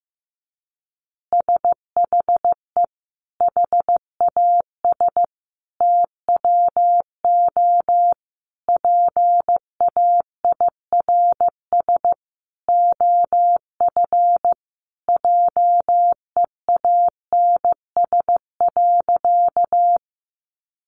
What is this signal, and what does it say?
Morse code, heard as a keyed tone, about 15 wpm, 700 Hz. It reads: SHE HAS TWO PAIRS OF JEANS.
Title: Additional Easy Phrases 15wpm